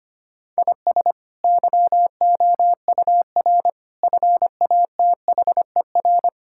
Morse code, heard as a keyed tone, about 25 wpm, 700 Hz. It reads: IS YOUR FATHER
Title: Sentences from Top 1600 Words 25wpm